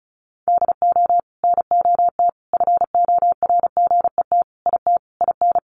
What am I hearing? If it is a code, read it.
Morse code, heard as a keyed tone, about 34 wpm, 700 Hz. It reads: DO NOT FORGET IT IN
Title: Sentences from Top 1200 Words 35wpm